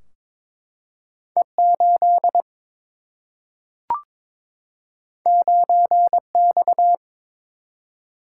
Morse code, heard as a keyed tone, about 22 wpm, 700 Hz. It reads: E 8 9 X
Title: Single-Letter Single-Number 22wpm